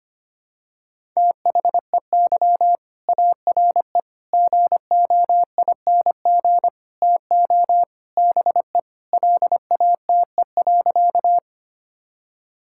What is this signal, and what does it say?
Morse code, heard as a keyed tone, about 25 wpm, 700 Hz. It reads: THEY ARE GOING TO BE LATE.
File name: practice / 25wpm / Sentences with 4-Word Phrases - 25wpm